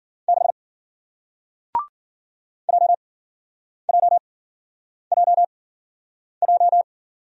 Morse code, heard as a keyed tone, about 53 wpm, spaced as for 9 wpm, 700 Hz. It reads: 6 JJJJ